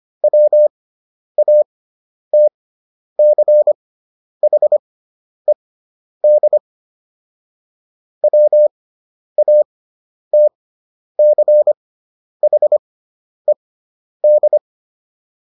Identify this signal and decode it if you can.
Morse code, heard as a keyed tone, about 25 wpm, spaced as for 10 wpm, 600 Hz. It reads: WATCHED WATCHED